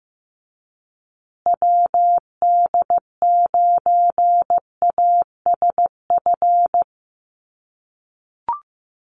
Morse code, heard as a keyed tone, about 15 wpm, 700 Hz. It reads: WD9ASF
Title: USA Call Signs Repeated 2x 15wpm